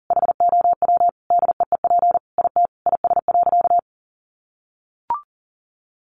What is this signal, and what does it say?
Morse code, heard as a keyed tone, about 40 wpm, 700 Hz. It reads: HOW DEEP IT IS.
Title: Sentences from Top 1100 Words 40wpm